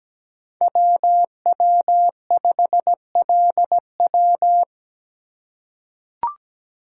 Morse code, heard as a keyed tone, about 17 wpm, 700 Hz. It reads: WW5LW
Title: USA Call Signs - 5 Characters - 3x Character Spacing - 17wpm